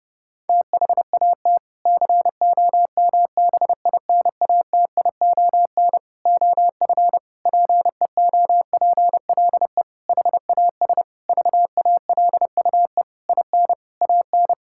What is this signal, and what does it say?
Morse code, heard as a keyed tone, about 32 wpm, 700 Hz. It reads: THAT COMBINATION OF PEOPLE HAS VALUE IN AN
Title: Sentences from Top 1000 Words - Review Part 6 - 30wpm